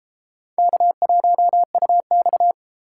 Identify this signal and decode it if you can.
Morse code, heard as a keyed tone, about 30 wpm, 700 Hz. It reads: K1UX